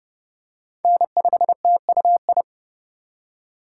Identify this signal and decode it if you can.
Morse code, heard as a keyed tone, about 30 wpm, 700 Hz. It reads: N5TUI
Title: USA Call Signs - 4x Character Spacing - 30wpm